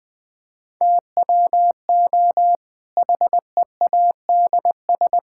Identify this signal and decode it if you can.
Morse code, heard as a keyed tone, about 20 wpm, 700 Hz. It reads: TWO HEADS
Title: Difficult Sentences with Hints - 20wpm